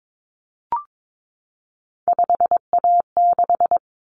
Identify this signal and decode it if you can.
Morse code, heard as a keyed tone, about 22 wpm, 700 Hz. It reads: 5A6
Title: Common Cut Numbers - 3-Digit 22wpm